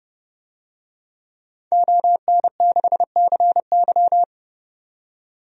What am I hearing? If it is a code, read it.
Morse code, heard as a keyed tone, about 30 wpm, 700 Hz. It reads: ON6CY
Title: DXCC-100 Call Signs - 5 Characters - 30wpm